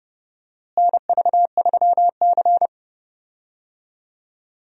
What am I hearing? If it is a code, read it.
Morse code, heard as a keyed tone, about 30 wpm, 700 Hz. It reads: NV3C
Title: USA Call Signs - 4 Characters - Repeated 2x - 30wpm